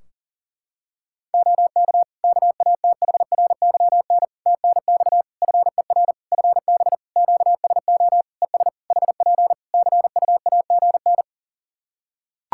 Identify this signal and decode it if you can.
Morse code, heard as a keyed tone, about 40 wpm, 700 Hz. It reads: OK KATHRYN TNX FER FB QSO ES HP CUAGN
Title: Practice QSO - Exchange 4 - Ending - 40wpm